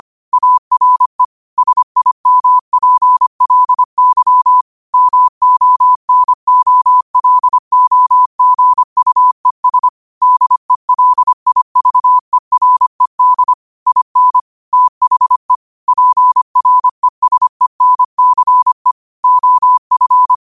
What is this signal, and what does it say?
Morse code, heard as a keyed tone, about 25 wpm, 1000 Hz. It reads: ARE SIMPLY MONOLOGUES DELIVERED IN THE PRESENCE OF
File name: qotdfiles / 25wpm